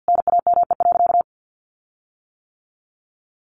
Morse code, heard as a keyed tone, about 50 wpm, 700 Hz. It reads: NAME.
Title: Sentences from Top 1500 Words 50wpm